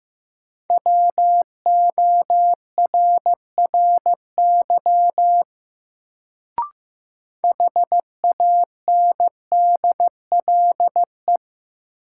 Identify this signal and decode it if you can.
Morse code, heard as a keyed tone, about 15 wpm, 700 Hz. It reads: WORRY HANDLE
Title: Sentences from Top 1100 Words - New Word Drill Alt - 15wpm